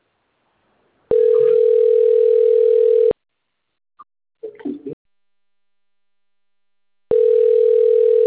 Ringback tone (North America), 440 + 480 Hz.